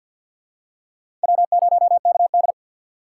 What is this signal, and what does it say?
Morse code, heard as a keyed tone, about 48 wpm, 700 Hz. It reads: W0KD
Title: USA Call Signs - 4 Characters - 6x Character Spacing - 50wpm